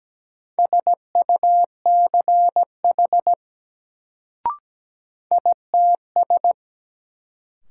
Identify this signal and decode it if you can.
Morse code, heard as a keyed tone, about 17 wpm, 700 Hz. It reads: SUCH ITS
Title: Sets of 2 Words - New Word Drill - 17wpm